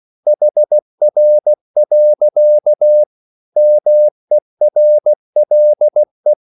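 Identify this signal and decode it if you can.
Morse code, heard as a keyed tone, about 16 wpm, 600 Hz. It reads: HR. MERLE
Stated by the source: Slow (QRS) Morse code at 16 wpm, from CW med Gnister.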